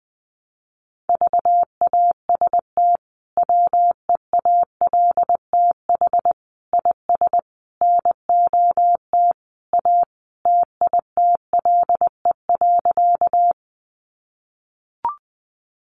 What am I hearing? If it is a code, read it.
Morse code, heard as a keyed tone, about 20 wpm, 700 Hz. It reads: VAST WEALTH IS NOT A TITLE.
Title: Sentences from Top 1300 Words 20wpm